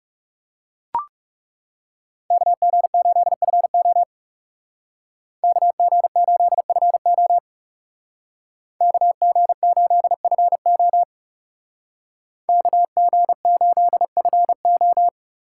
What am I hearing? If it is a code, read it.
Morse code, heard as a keyed tone, about 39 wpm, 700 Hz. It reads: KG8FO KG8FO KG8FO KG8FO